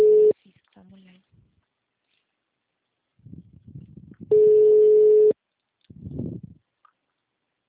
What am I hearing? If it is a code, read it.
Ringback tone (Europe, 425 Hz), with speech.